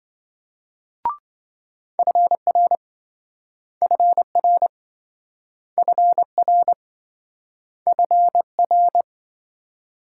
Morse code, heard as a keyed tone, about 28 wpm, 700 Hz. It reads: FR FR FR FR